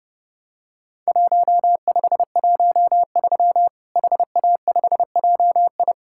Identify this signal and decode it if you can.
Morse code, heard as a keyed tone, about 30 wpm, 700 Hz. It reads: 1513 HA5JI
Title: Worked All Europe DX Contest - QTC Exchange - 30wpm